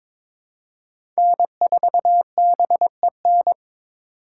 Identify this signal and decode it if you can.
Morse code, heard as a keyed tone, about 21 wpm, 700 Hz. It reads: N4BEN
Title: USA Call Signs - 5 Characters - 4x Character Spacing - 22wpm